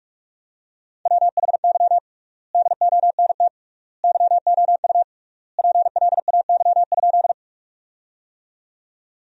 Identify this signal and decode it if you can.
Morse code, heard as a keyed tone, about 41 wpm, 700 Hz. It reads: WHY DONT YOU PLAY?
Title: 3x Word-Spacing - Sentences from Top 300 Words - 45wpm